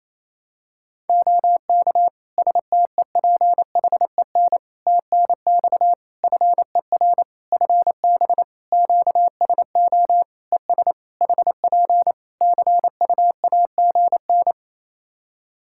Morse code, heard as a keyed tone, about 28 wpm, 700 Hz. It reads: OK STEPHEN TNX FER FB QSO ES HP CUAGN
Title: Practice QSO - Exchange 4 - Ending - 28wpm